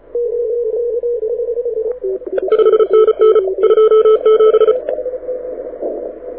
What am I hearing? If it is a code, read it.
Morse code, heard as a keyed tone, about 34 wpm, 1310 Hz. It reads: 5NN 27
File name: audio zawody / IARU 2015 CW SN7Q